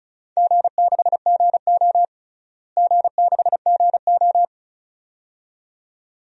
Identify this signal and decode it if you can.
Morse code, heard as a keyed tone, about 36 wpm, 700 Hz. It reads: G6GO G6GO
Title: DXCC-100 Call Signs - 4 Characters - Repeated 2x - 35wpm